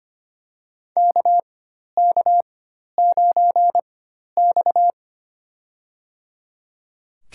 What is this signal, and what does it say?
Morse code, heard as a keyed tone, about 25 wpm, spaced as for 12 wpm, 700 Hz. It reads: KK9X
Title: USA Call Signs - 4 Characters - 4x Character Spacing - 25wpm